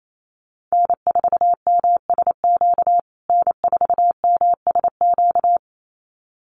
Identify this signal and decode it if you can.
Morse code, heard as a keyed tone, about 28 wpm, 700 Hz. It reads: N4MSQ N4MSQ
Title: USA Call Signs Repeated 2x 28wpm